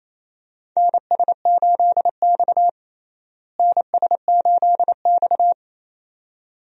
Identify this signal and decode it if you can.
Morse code, heard as a keyed tone, about 28 wpm, 700 Hz. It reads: NS8X NS8X
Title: USA Call Signs - 4 Characters - Repeated 2x - 28wpm